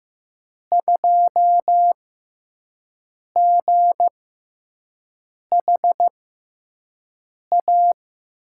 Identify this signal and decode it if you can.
Morse code, heard as a keyed tone, about 15 wpm, spaced as for 5 wpm, 700 Hz. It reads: 2GHA
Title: USA Call Signs - 6 Characters - 6x Character Spacing - 15wpm